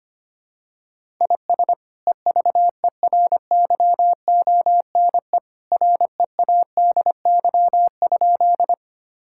Morse code, heard as a keyed tone, about 25 wpm, 700 Hz. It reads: IS EVERYONE READY?
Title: Sentences from Top 1000 Words - Review Part 5 - 25wpm